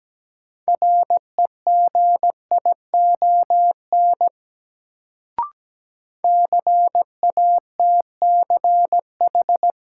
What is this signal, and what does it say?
Morse code, heard as a keyed tone, about 17 wpm, 700 Hz. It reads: REGION CATCH
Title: Sentences from Top 1500 Words - New Word Drill - 17wpm